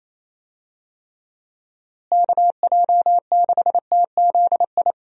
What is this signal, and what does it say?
Morse code, heard as a keyed tone, about 27 wpm, 700 Hz. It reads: KJ6TZI
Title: USA Call Signs - 6 Characters - Repeated 2x - 28wpm